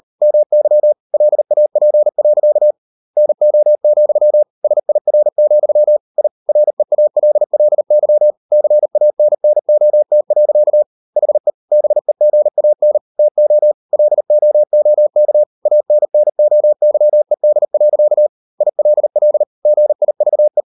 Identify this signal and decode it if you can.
Morse code, heard as a keyed tone, about 38 wpm, 600 Hz. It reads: MY LAP. NO, SIR, I REALLY CANNOT. HE BEGAN TO LOOK ANNOYED. ILL GIVE